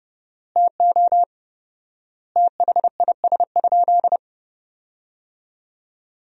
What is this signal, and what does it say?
Morse code, heard as a keyed tone, about 30 wpm, 700 Hz. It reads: TO THIS?
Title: 4x Word-Spacing - Sentences from Top 100 Words - 30wpm